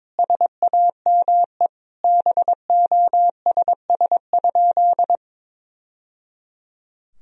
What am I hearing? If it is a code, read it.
Morse code, heard as a keyed tone, about 22 wpm, 700 Hz. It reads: SAME BOSS?
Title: Sentences from Top 1000 Words - Review Part 4 - 22wpm